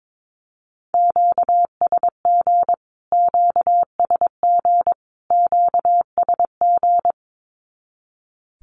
Morse code, heard as a keyed tone, about 22 wpm, 700 Hz. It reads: QSG QSG QSG
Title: Q-Codes Repeated 3x 22wpm